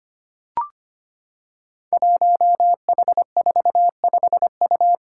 Morse code, heard as a keyed tone, about 25 wpm, 700 Hz. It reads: 1H45U
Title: Five Challenging Letter-Number Combinations 25wpm